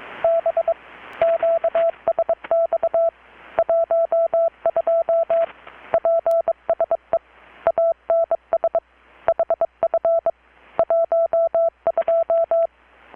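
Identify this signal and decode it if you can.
Morse code, heard as a keyed tone, about 22 wpm, 670 Hz. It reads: B QSX 12 PSE ANS HF 12
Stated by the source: Morse from KPH's Night of Nights XX maritime radiotelegraphy event, received in Chiapas, Mexico about 3,800 km away on a Kenwood R-600 with a grounded 10 m long-wire antenna.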